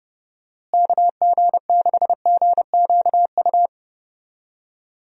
Morse code, heard as a keyed tone, about 30 wpm, 700 Hz. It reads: KG6GQU
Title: USA Call Signs - 6 Characters - Repeated 3x - 30wpm